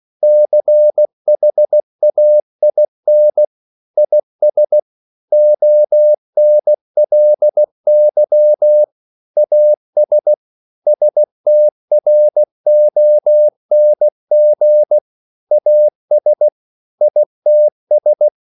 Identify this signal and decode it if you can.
Morse code, heard as a keyed tone, about 16 wpm, 600 Hz. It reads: CHAIN IS ONLY AS STRONG AS ITS
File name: CW / Practice / EnglishSayings16